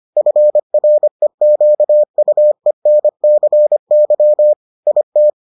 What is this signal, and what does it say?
Morse code, heard as a keyed tone, about 25 wpm, 600 Hz. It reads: FREQUENCY IT